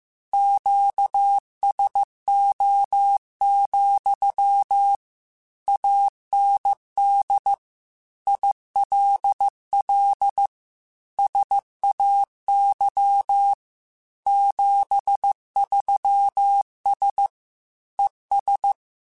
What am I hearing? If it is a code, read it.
Morse code, heard as a keyed tone, about 15 wpm, 800 Hz. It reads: QSO, AND ILL SAY 73S ES